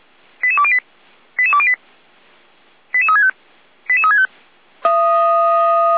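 Five-tone selective call (ZVEI), heard as quick runs of five tones, tones of about 70 ms. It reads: 88208 88208 88365 88365
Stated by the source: ZVEI five-tone call 88207 received by SDR, fire brigade Orlamünde.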